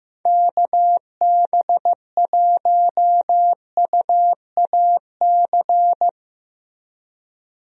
Morse code, heard as a keyed tone, about 14 wpm, 700 Hz. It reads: KB1UAC
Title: USA Call Signs - 6 Characters - Repeated 3x - 15wpm